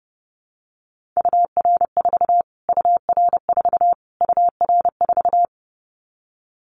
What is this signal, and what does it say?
Morse code, heard as a keyed tone, about 30 wpm, 690 Hz. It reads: UR4 UR4 UR4
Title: Callsign Prefixes for Top 25 DXCC - Repeated 3x - 30wpm